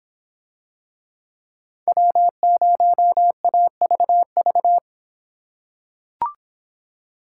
Morse code, heard as a keyed tone, about 25 wpm, 700 Hz. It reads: W0AVV